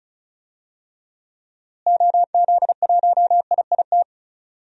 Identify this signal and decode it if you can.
Morse code, heard as a keyed tone, about 34 wpm, 700 Hz. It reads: OZ1IIT
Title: DXCC-100 Call Signs - 6 Characters - Repeated 3x - 35wpm